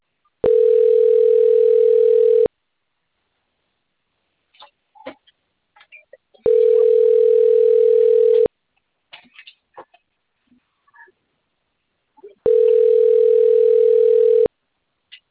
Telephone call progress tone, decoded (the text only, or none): ringback tone